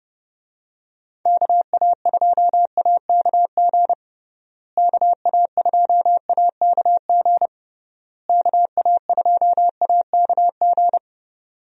Morse code, heard as a keyed tone, about 29 wpm, 700 Hz. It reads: KA2AKG KA2AKG KA2AKG
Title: USA Call Signs - 6 Characters - Repeated 3x - 30wpm